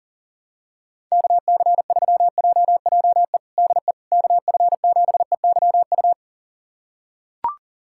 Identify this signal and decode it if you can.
Morse code, heard as a keyed tone, about 40 wpm, 700 Hz. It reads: KC3JJE DE KF7EYU
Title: USA Callsign DE Callsign 40wpm